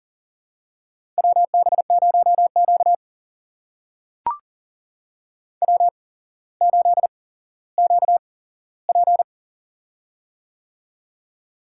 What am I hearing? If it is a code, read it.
Morse code, heard as a keyed tone, about 40 wpm, spaced as for 17 wpm, 700 Hz. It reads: WB0Q W8QP
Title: USA Call Signs - 4 Characters - 8x Character Spacing - 40wpm